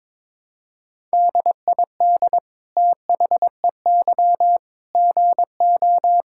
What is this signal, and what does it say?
Morse code, heard as a keyed tone, about 22 wpm, 700 Hz. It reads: DID THEY GO
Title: Sentences from Top 1000 Words - Review Part 5 - 22wpm